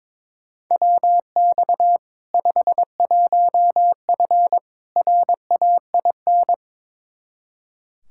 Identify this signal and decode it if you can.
Morse code, heard as a keyed tone, about 22 wpm, 700 Hz. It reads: WX 51F RAIN
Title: K1USN SST QSO Practice 22wpm